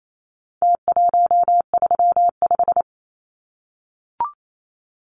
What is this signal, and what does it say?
Morse code, heard as a keyed tone, about 28 wpm, 700 Hz. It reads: T135
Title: Numbers - Contest Style - 4-Digit 28wpm